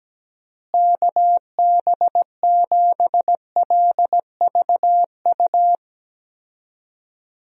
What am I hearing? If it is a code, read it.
Morse code, heard as a keyed tone, about 17 wpm, 700 Hz. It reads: KB7LVU
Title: USA Call Signs - 6 Characters - Repeated 3x - 17wpm